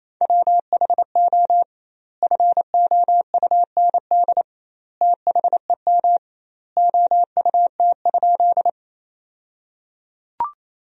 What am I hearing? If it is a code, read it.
Morse code, heard as a keyed tone, about 28 wpm, 700 Hz. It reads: WHO FOUND THEM OUT?
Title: 2x Word-Spacing - Sentences from Top 300 Words - 28wpm